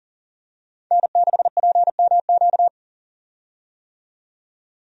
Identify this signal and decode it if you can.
Morse code, heard as a keyed tone, about 40 wpm, 700 Hz. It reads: N6PMQ
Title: USA Call Signs - 5 Characters - 40wpm